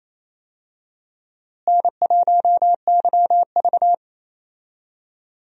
Morse code, heard as a keyed tone, about 28 wpm, 700 Hz. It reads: N1YV
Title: USA Call Signs - 4 Characters - Repeated 2x - 28wpm